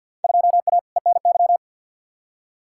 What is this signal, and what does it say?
Morse code, heard as a keyed tone, about 47 wpm, 700 Hz. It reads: 2A ENY
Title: Field Day Exchanges - 50wpm